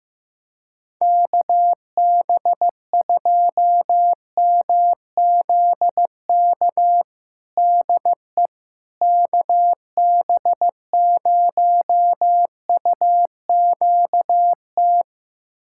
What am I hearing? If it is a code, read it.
Morse code, heard as a keyed tone, about 16 wpm, 700 Hz. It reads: KB2MZK DE KB0UQT
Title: USA Callsign DE Callsign 15wpm